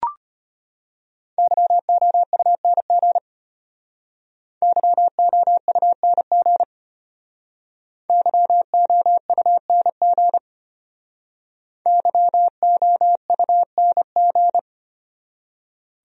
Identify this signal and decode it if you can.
Morse code, heard as a keyed tone, about 32 wpm, 700 Hz. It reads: YOUNG YOUNG YOUNG YOUNG